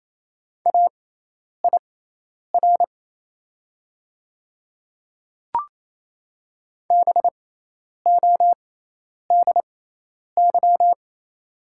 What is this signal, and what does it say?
Morse code, heard as a keyed tone, about 27 wpm, spaced as for 10 wpm, 700 Hz. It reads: AIR BODY
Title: 6x Character Spacing - Top 500 Words - 28wpm